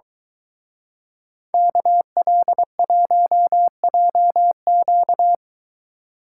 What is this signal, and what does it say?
Morse code, heard as a keyed tone, about 23 wpm, 700 Hz. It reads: KL1JQ